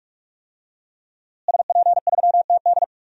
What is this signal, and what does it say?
Morse code, heard as a keyed tone, about 46 wpm, 700 Hz. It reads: SP3TD
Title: DXCC-100 Call Signs - 5 Characters - Repeated 3x - 45wpm